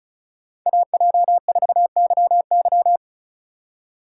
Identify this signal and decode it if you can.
Morse code, heard as a keyed tone, about 36 wpm, 700 Hz. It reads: AJ4YY
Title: USA Call Signs - 5 Characters - 4x Character Spacing - 35wpm